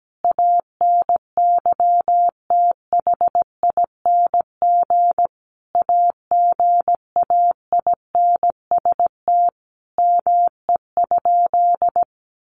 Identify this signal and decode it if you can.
Morse code, heard as a keyed tone, about 17 wpm, 700 Hz. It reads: ANYTHING AGAINST ME?
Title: Sentences from Top 1000 Words 17wpm